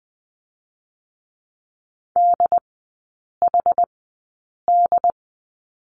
Morse code, heard as a keyed tone, about 20 wpm, spaced as for 8 wpm, 700 Hz. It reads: DHD
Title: 087 Lesson 15e - Number 5 - Three Characters 20wpm